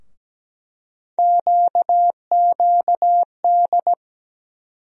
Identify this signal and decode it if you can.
Morse code, heard as a keyed tone, about 17 wpm, 700 Hz. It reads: QQD
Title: USA Call Sign Postfixes - 3 Letters 17wpm